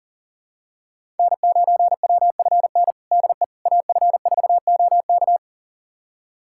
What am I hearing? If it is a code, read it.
Morse code, heard as a keyed tone, about 37 wpm, 700 Hz. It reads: N9WFN DE AF4OK
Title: USA Callsign DE Callsign - 5 Characters - 40wpm